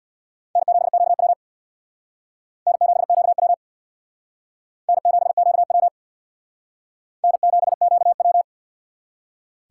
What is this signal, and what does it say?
Morse code, heard as a keyed tone, about 64 wpm, 700 Hz. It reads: N7QW N7QW N7QW N7QW